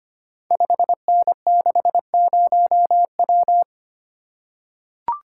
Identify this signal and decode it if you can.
Morse code, heard as a keyed tone, about 25 wpm, 700 Hz. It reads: 5N60W
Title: Code Groups - 5 Letter:Numbers - 25wpm